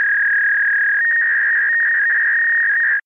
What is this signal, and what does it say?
Fax machine handshake.